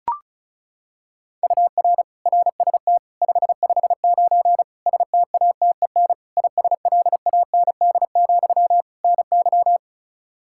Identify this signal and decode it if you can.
Morse code, heard as a keyed tone, about 34 wpm, 700 Hz. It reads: UR RST 559 STATEN ISLAND, NY
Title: Practice QSO - Exchange 1 - Beginning - 35wpm